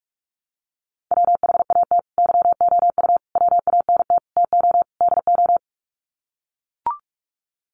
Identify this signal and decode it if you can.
Morse code, heard as a keyed tone, about 44 wpm, 700 Hz. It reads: WHAT YOU WANT TO DO
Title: Sets of 5 Words Encore 45wpm